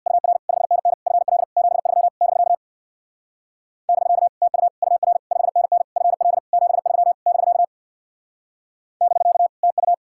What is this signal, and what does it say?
Morse code, heard as a keyed tone, about 65 wpm, 700 Hz. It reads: UR 5NN FL 73 <BK> <BK> TU UR 5NN FL 73 <BK> <BK> TU